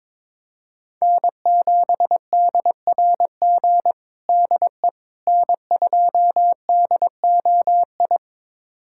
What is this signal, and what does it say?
Morse code, heard as a keyed tone, about 22 wpm, 700 Hz. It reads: N7DRG DE N2DOI
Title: USA Callsign DE Callsign - 5 Characters - 22wpm